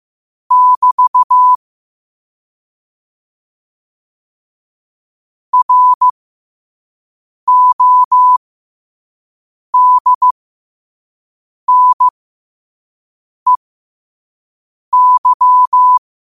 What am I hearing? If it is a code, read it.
Morse code, heard as a keyed tone, about 15 wpm, spaced as for 5 wpm, 1000 Hz. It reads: <BT> RODNEY
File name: qotdfiles / 5wpm